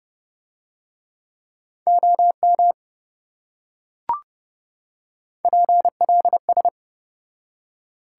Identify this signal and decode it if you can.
Morse code, heard as a keyed tone, about 29 wpm, 700 Hz. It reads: OM PLS